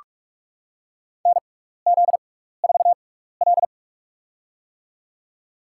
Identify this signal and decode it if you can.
Morse code, heard as a keyed tone, about 46 wpm, spaced as for 16 wpm, 700 Hz. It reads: NZ4L
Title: USA Call Signs - 4 Characters - 6x Character Spacing - 45wpm